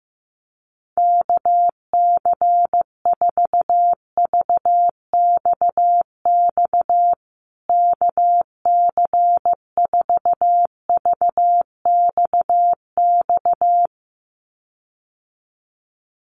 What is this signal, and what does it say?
Morse code, heard as a keyed tone, about 15 wpm, 700 Hz. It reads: KC4VXX KC4VXX
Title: USA Call Signs Repeated 2x 15wpm